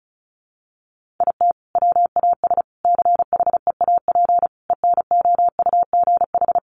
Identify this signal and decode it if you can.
Morse code, heard as a keyed tone, about 34 wpm, 700 Hz. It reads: IT WAS CHEAP ENOUGH
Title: Sentences from Top 1200 Words 35wpm